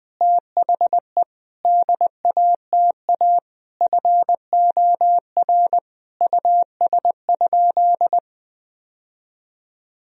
Morse code, heard as a keyed tone, about 20 wpm, 700 Hz. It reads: THE DATA FOR US?
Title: Sentences from Top 1000 Words - Review Part 3 - 20wpm